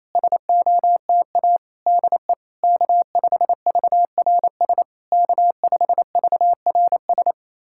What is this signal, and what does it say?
Morse code, heard as a keyed tone, about 28 wpm, 700 Hz. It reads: SOTA DE K5VRS K5VRS
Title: SOTA QSO Practice 28wpm